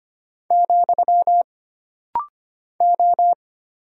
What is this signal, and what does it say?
Morse code, heard as a keyed tone, about 25 wpm, 700 Hz. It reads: , O
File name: Single Character (Letter:Number:Punctuation:Prosign) - Rapid-Fire - 25wpm